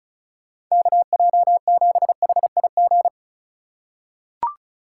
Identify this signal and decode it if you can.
Morse code, heard as a keyed tone, about 36 wpm, 700 Hz. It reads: KJ7HIG
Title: USA Call Signs - 6 Characters - 2x Character Spacing - 35wpm